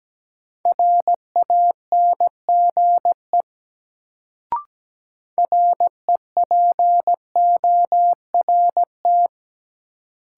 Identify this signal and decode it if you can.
Morse code, heard as a keyed tone, about 17 wpm, 700 Hz. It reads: RANGE REPORT
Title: Sentences from Top 900 Words - New Word Drill - 17wpm